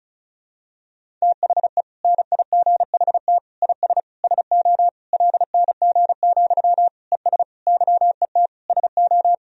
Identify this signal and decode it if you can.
Morse code, heard as a keyed tone, about 36 wpm, 700 Hz. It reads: THE NIGHT IS SO LNG, ES YET SO
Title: Sentences from Top 500 Words and Abbreviations 35wpm